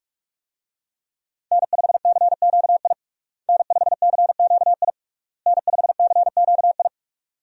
Morse code, heard as a keyed tone, about 46 wpm, 700 Hz. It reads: N5CQI N5CQI N5CQI
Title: USA Call Signs - 5 Characters - Repeated 3x - 45wpm